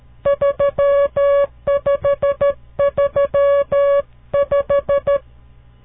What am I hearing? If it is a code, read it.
Morse code, heard as a keyed tone, about 13 wpm, 560 Hz. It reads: 3535
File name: CW Learning Programs / farnsworth / unit 5